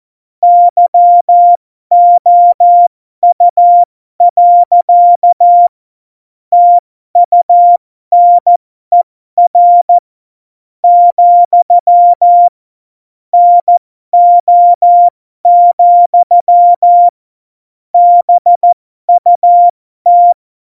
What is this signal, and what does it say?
Morse code, heard as a keyed tone, about 14 wpm, spaced as for 11 wpm, 700 Hz. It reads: YOU. TUNER, NO, BUT
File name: Morsekurs / Witze